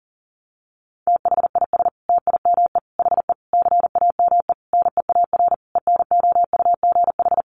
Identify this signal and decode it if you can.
Morse code, heard as a keyed tone, about 40 wpm, 700 Hz. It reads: THIS TIME HE CAME NEAR ENOUGH
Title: Sentences from Top 1200 Words 40wpm